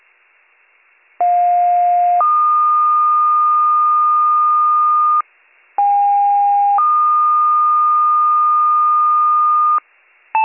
Two-tone page, heard as a one-second tone, then a three-second tone, about 710 then 1220 Hz, about 810 then 1220 Hz. It is heard from 1.2 to 9.8 s.